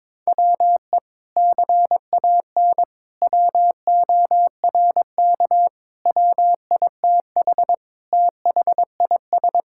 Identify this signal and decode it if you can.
Morse code, heard as a keyed tone, about 22 wpm, 700 Hz. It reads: WE CAN WORK WITH THIS
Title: Sentences from Top 1000 Words - Review Part 5 - 22wpm